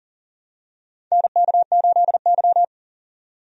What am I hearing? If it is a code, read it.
Morse code, heard as a keyed tone, about 40 wpm, 700 Hz. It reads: NK8Y
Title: USA Call Signs - 4 Characters - 40wpm